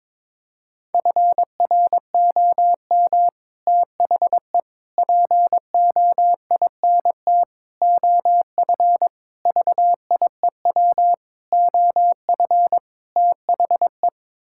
Morse code, heard as a keyed tone, about 22 wpm, 700 Hz. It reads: FROM THE POINT OF VIEW OF THE
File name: practice / 22wpm / Sets of 7 Words - 22wpm